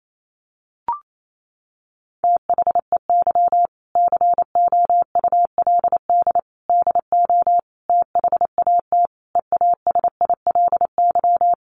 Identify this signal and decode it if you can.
Morse code, heard as a keyed tone, about 28 wpm, 700 Hz. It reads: THEY COULD DO THAT EASILY
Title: Sentences from Top 900 Words 28wpm